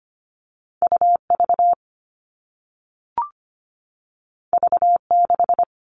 Morse code, heard as a keyed tone, about 25 wpm, 700 Hz. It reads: U V V 6
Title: All Cut Numbers - 2-Digit 25wpm